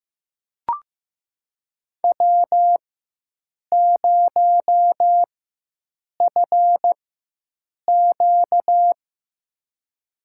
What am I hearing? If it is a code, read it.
Morse code, heard as a keyed tone, about 15 wpm, spaced as for 7 wpm, 700 Hz. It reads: W0FQ
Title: USA Call Signs - 4x Character Spacing - 15wpm